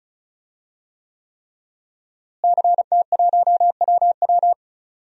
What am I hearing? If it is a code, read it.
Morse code, heard as a keyed tone, about 34 wpm, 700 Hz. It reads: CT1WW
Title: DXCC-100 Call Signs - 5 Characters - Repeated 3x - 35wpm